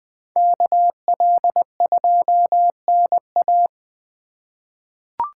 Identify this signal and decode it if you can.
Morse code, heard as a keyed tone, about 20 wpm, 700 Hz. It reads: KL2NA
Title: USA Call Signs - 5 Characters - 4x Character Spacing - 20wpm